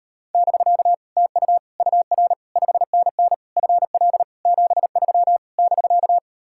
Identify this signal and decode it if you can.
Morse code, heard as a keyed tone, about 38 wpm, 700 Hz. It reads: <BK> TU UR 5NN FL 73 <BK>